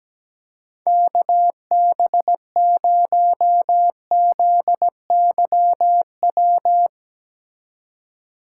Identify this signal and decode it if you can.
Morse code, heard as a keyed tone, about 17 wpm, 700 Hz. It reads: KB0ZYW
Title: Speed Racing - USA 2x3 Call Signs 17wpm